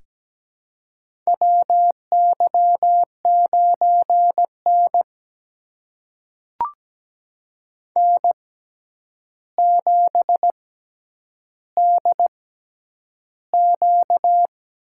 Morse code, heard as a keyed tone, about 17 wpm, spaced as for 9 wpm, 700 Hz. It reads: WY9N N7DQ
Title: USA Call Signs - 4 Characters - 6x Character Spacing - 17wpm